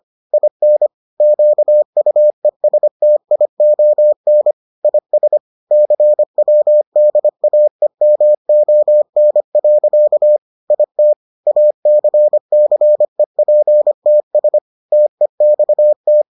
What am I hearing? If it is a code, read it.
Morse code, heard as a keyed tone, about 25 wpm, 600 Hz. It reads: IN QUESTION IS CWDAEMON. IT ACCEPTS TEXT